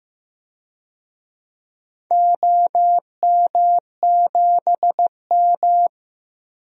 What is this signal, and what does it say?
Morse code, heard as a keyed tone, about 15 wpm, 700 Hz. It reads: OM7M
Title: DXCC-100 Call Signs - 4 Characters - 15wpm